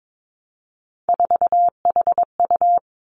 Morse code, heard as a keyed tone, about 22 wpm, 700 Hz. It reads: 4HU
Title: Three Challenging Letter-Number Combinations 22wpm